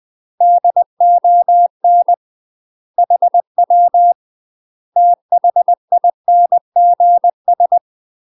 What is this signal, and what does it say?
Morse code, heard as a keyed tone, about 20 wpm, 700 Hz. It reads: DON HW THINGS
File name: UK MASTER QSOs TRAINING / 20WPM / UK MASTER QSOs 0027